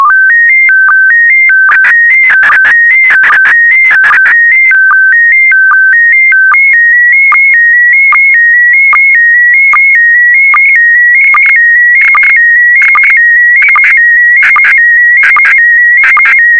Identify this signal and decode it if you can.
SSTV (slow-scan television), PD 160.